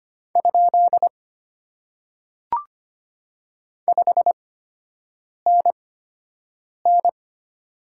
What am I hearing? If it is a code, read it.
Morse code, heard as a keyed tone, about 25 wpm, spaced as for 7 wpm, 700 Hz. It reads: ? 5NN
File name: Top 90 QSO Elements - 8x Character Spacing - 25wpm